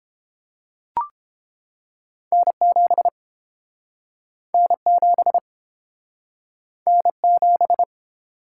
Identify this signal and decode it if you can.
Morse code, heard as a keyed tone, about 32 wpm, 700 Hz. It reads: N7 N7 N7